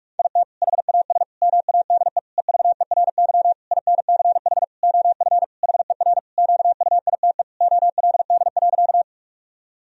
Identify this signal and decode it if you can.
Morse code, heard as a keyed tone, about 41 wpm, 700 Hz. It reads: IT HAS MADE EVERY INCH OF HER QUITE OLD.
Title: Sentences from Top 1000 Words - Review Part 3 - 45wpm